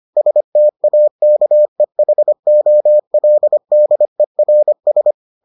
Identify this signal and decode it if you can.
Morse code, heard as a keyed tone, about 25 wpm, 600 Hz. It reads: STAKEHOLDERS